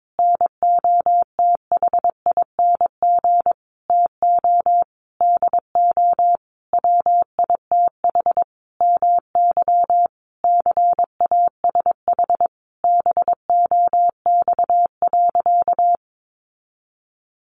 Morse code, heard as a keyed tone, about 22 wpm, 700 Hz. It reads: NOTHING TO DO WITH MY CASH BOX.
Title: Sentences from Top 700 Words 22wpm